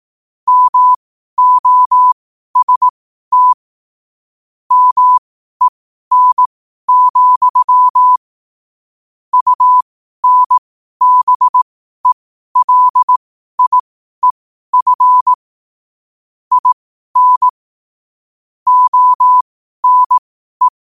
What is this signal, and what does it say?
Morse code, heard as a keyed tone, about 18 wpm, spaced as for 12 wpm, 1000 Hz. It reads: MOST MEN, UNBELIEF IN ONE